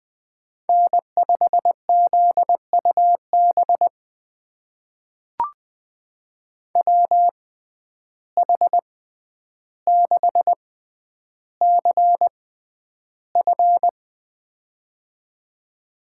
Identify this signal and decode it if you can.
Morse code, heard as a keyed tone, about 21 wpm, spaced as for 10 wpm, 700 Hz. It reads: N5ZUB WH6CF
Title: USA Call Signs - 5 Characters - 6x Character Spacing - 20wpm